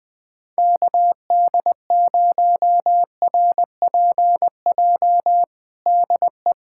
Morse code, heard as a keyed tone, about 20 wpm, 700 Hz. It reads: KD0RPJ DE
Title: USA Callsign DE Callsign - 6 Characters - 20wpm